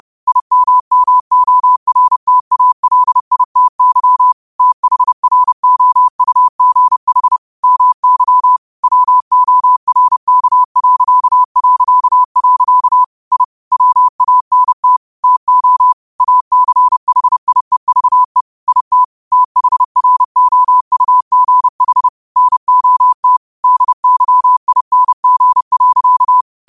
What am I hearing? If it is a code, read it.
Morse code, heard as a keyed tone, about 30 wpm, 1000 Hz. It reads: IMMORTALITY THROUGH MY WORK... I WANT TO ACHIEVE IT THROUGH NOT DYING.
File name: qotdfiles / 30wpm